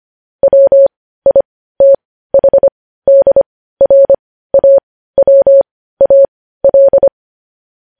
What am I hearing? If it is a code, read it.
Morse code, heard as a keyed tone, about 25 wpm, spaced as for 15 wpm, 550 Hz. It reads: WITHDRAWAL
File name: podcast / News headlines-1x-25x15